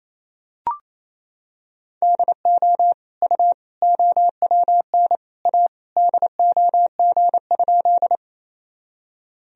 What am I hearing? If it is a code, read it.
Morse code, heard as a keyed tone, about 28 wpm, 700 Hz. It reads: DO U OWN A DOG?
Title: Sentences from Top 400 Words and Abbreviations 28wpm